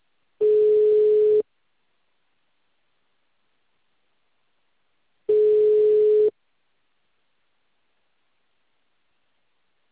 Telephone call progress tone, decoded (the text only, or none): ringback tone